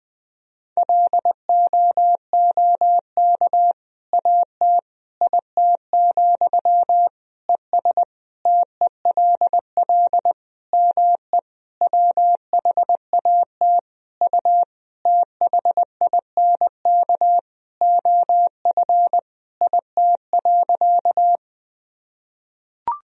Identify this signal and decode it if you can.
Morse code, heard as a keyed tone, about 21 wpm, 700 Hz. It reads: LOOK AT IT, ES TELL ME WHAT U THINK OF IT.
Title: Sentences from Top 200 Words and Abbreviations 20wpm